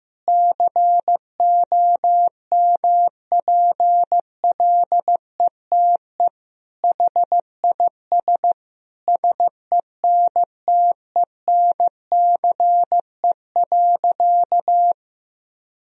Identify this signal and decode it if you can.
Morse code, heard as a keyed tone, about 15 wpm, 700 Hz. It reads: COMPLETE HIS SENTENCE.
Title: Sentences from Top 1000 Words - Review Part 1 - 15wpm